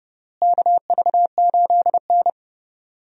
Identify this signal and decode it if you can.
Morse code, heard as a keyed tone, about 30 wpm, 700 Hz. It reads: KV8N
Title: USA Call Signs - 4 Characters - Repeated 3x - 30wpm